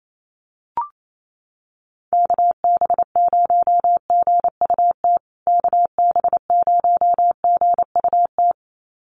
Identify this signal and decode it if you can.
Morse code, heard as a keyed tone, about 28 wpm, 700 Hz. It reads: KB0GUT KB0GUT
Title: USA Call Signs Repeated 2x 28wpm